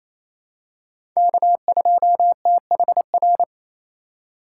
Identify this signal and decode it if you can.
Morse code, heard as a keyed tone, about 27 wpm, 700 Hz. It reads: K2THR
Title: USA Call Signs - 2x Character Spacing - 28wpm